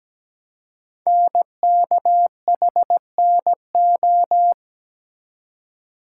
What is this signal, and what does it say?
Morse code, heard as a keyed tone, about 17 wpm, 700 Hz. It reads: NKHNO
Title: Code Groups - 5 Letters - 17wpm